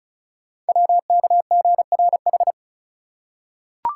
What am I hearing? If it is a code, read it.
Morse code, heard as a keyed tone, about 36 wpm, 690 Hz. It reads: WKGRH
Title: Code Groups - 5 Letters - 35wpm